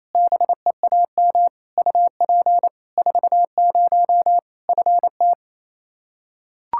Morse code, heard as a keyed tone, about 28 wpm, 700 Hz. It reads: BEAM UP 40 FT
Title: K1USN SST QSO Practice 28wpm